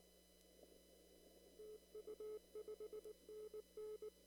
Morse code, heard as a keyed tone, about 19 wpm, 430 Hz. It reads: TU 5NN
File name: CW / kt5j wpx cw-op-k5pi